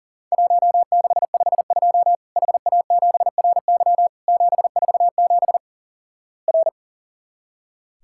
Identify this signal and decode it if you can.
Morse code, heard as a keyed tone, about 44 wpm, 700 Hz. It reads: 1652 HA7RY 747 R
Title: Worked All Europe DX Contest - QTC Exchange - 40wpm